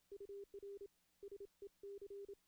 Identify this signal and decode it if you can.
Morse code, heard as a keyed tone, about 26 wpm, 400 Hz. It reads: UR SEC
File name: contests / CW-SS-k5tr- -1600